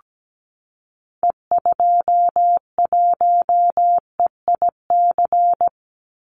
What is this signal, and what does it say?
Morse code, heard as a keyed tone, about 17 wpm, 700 Hz. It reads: E21EIC
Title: Call Signs Repeated 3x - 17wpm